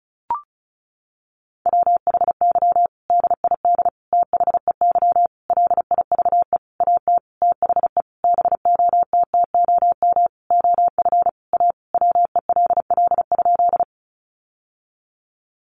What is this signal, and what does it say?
Morse code, heard as a keyed tone, about 34 wpm, 700 Hz. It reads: WHY DID THEY LIVE AT THE BOTTOM OF A WELL?
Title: Sentences from Top 900 Words 35wpm